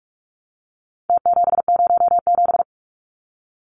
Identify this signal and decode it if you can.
Morse code, heard as a keyed tone, about 44 wpm, 700 Hz. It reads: T707
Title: Numbers - Contest Style - 4-Digit 45wpm